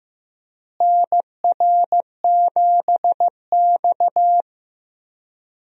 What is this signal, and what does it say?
Morse code, heard as a keyed tone, about 15 wpm, 700 Hz. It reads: NR7X
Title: USA Call Signs - 4 Characters - Repeated 3x - 15wpm